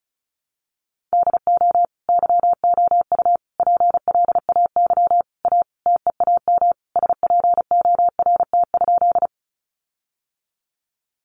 Morse code, heard as a keyed tone, about 34 wpm, 700 Hz. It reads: DO YOU PLAY A TEAM SPORT?
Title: Sentences from Top 1500 Words 35wpm